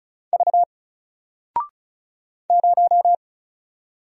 Morse code, heard as a keyed tone, about 36 wpm, 700 Hz. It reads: V 0
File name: Single-Letter-Number-35wpm